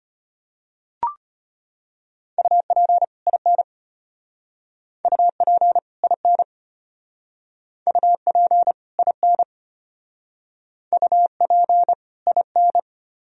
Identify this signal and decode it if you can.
Morse code, heard as a keyed tone, about 32 wpm, 700 Hz. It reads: UP IN UP IN UP IN UP IN